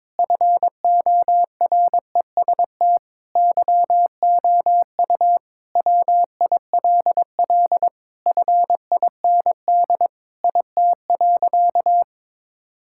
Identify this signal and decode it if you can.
Morse code, heard as a keyed tone, about 22 wpm, 700 Hz. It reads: FOREST YOU WILL FIND IT.
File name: practice / 22wpm / Sentences with 4-Word Phrases - 22wpm